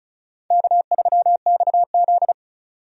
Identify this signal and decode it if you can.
Morse code, heard as a keyed tone, about 36 wpm, 700 Hz. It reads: K3XZ
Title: USA Call Signs - 4 Characters - 35wpm